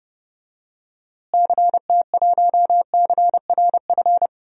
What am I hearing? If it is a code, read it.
Morse code, heard as a keyed tone, about 30 wpm, 700 Hz. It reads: CT1CRF
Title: DXCC-100 Call Signs - 6 Characters - Repeated 2x - 30wpm